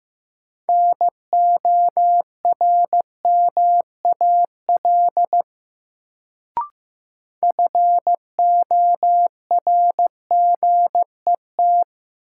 Morse code, heard as a keyed tone, about 15 wpm, 700 Hz. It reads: NORMAL FORGET
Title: Sentences from Top 1200 Words - New Word Drill - 15wpm